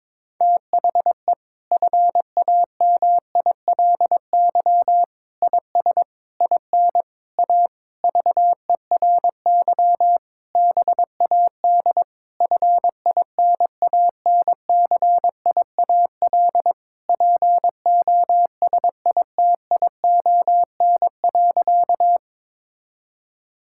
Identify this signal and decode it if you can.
Morse code, heard as a keyed tone, about 22 wpm, 700 Hz. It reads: THE FAMILY IS IN A VERY BAD FINANCIAL POSITION.
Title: Sentences from Top 1000 Words - Review Part 5 - 22wpm